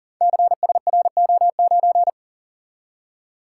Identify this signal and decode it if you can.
Morse code, heard as a keyed tone, about 42 wpm, 690 Hz. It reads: CSRO9
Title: Code Groups - 5 Letter:Numbers - 40wpm